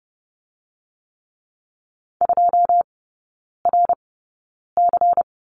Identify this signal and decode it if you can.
Morse code, heard as a keyed tone, about 30 wpm, spaced as for 9 wpm, 700 Hz. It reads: 2RC
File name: Lesson 16e - Letter c - Three Characters 30wpm